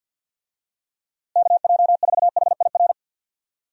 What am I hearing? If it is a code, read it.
Morse code, heard as a keyed tone, about 50 wpm, 700 Hz. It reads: KJ4HIR
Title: USA Call Signs - 6 Characters - Repeated 3x - 50wpm